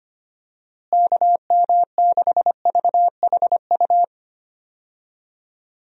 Morse code, heard as a keyed tone, about 25 wpm, 700 Hz. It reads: KM6VHU
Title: USA Call Signs - 6 Characters - Repeated 3x - 25wpm